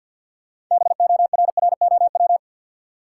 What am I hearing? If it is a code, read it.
Morse code, heard as a keyed tone, about 52 wpm, 700 Hz. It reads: BORROW
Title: Six Letter Words - 50wpm